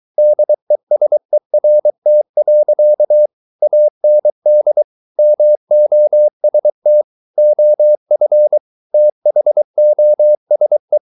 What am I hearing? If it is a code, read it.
Morse code, heard as a keyed tone, about 23 wpm, 600 Hz. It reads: DESERT. AND MOST OF THOSE